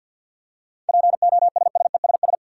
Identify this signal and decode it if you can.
Morse code, heard as a keyed tone, about 50 wpm, 700 Hz. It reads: POSSESS
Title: Seven Letter Words - 50wpm